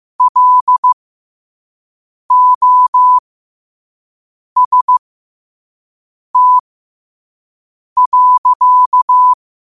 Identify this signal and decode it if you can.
Morse code, heard as a keyed tone, about 15 wpm, spaced as for 5 wpm, 1000 Hz. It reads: LOST.